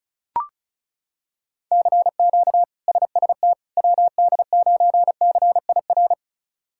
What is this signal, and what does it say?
Morse code, heard as a keyed tone, about 36 wpm, 700 Hz. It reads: CQ SST WD9CIR
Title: K1USN SST QSO Practice - 35wpm